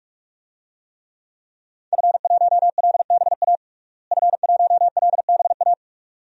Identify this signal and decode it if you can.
Morse code, heard as a keyed tone, about 46 wpm, 700 Hz. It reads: F1LBA F1LBA
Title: DXCC-100 Call Signs - 5 Characters - Repeated 2x - 45wpm